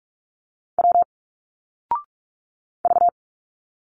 Morse code, heard as a keyed tone, about 46 wpm, 700 Hz. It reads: W V
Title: Single Letters 45wpm